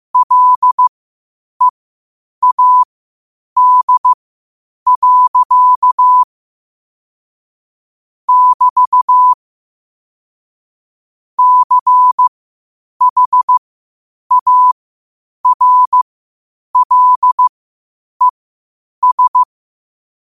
Morse code, heard as a keyed tone, about 15 wpm, spaced as for 8 wpm, 1000 Hz. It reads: LEAD. <BT> CHARLES
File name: qotdfiles / 8wpm